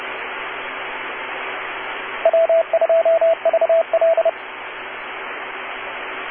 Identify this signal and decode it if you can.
Morse code, heard as a keyed tone, about 30 wpm, 660 Hz, with radio noise. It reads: W2VL